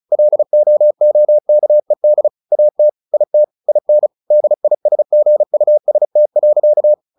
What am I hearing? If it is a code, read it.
Morse code, heard as a keyed tone, about 34 wpm, 600 Hz. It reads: LOOKED AT IT IN DISGUST.
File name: the old man and the sea / the old man and the sea 1 35wpm